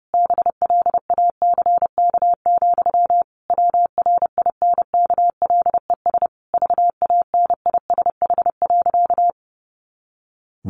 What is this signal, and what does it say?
Morse code, heard as a keyed tone, about 30 wpm, 700 Hz. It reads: BLACK, WRINKLES VANISH.
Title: A Sound Of Thunder 30wpm